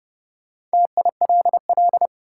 Morse code, heard as a keyed tone, about 30 wpm, 700 Hz. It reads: TILL
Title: Four Letter Words - 30wpm